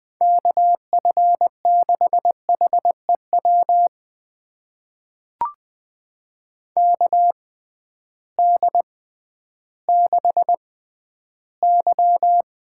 Morse code, heard as a keyed tone, about 20 wpm, spaced as for 12 wpm, 700 Hz. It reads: KF6HEW KD6Y
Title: USA Call Signs - 6 Characters - 6x Character Spacing - 20wpm